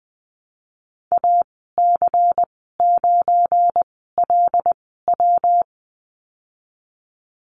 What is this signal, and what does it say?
Morse code, heard as a keyed tone, about 20 wpm, spaced as for 15 wpm, 700 Hz. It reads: AC9LW
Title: 193 Lesson 31g - Letter v - Call Signs 20wpm